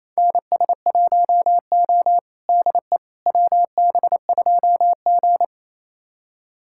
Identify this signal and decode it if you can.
Morse code, heard as a keyed tone, about 28 wpm, 700 Hz. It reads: NS1O DE WB2G
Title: USA Callsign DE Callsign - 4 Characters - 28wpm